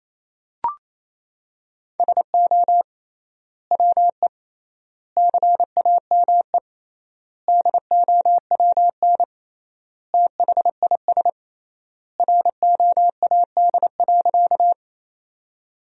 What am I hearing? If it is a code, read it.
Morse code, heard as a keyed tone, about 28 wpm, 700 Hz. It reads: SO WE CAME DOWN THIS ROAD.
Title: 3x Word-Spacing - Sentences from Top 400 Words - 28wpm